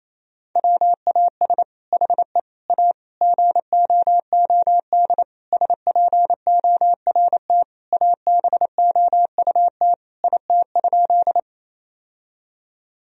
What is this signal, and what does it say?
Morse code, heard as a keyed tone, about 28 wpm, 700 Hz. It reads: WAS HE A GOOD SPORT ABOUT IT?
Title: Sentences from Top 1000 Words - Review Part 4 - 28wpm